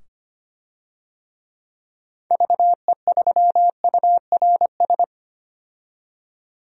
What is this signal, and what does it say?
Morse code, heard as a keyed tone, about 25 wpm, 700 Hz. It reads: VE3URS
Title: DXCC-100 Call Signs - 6 Characters - 25wpm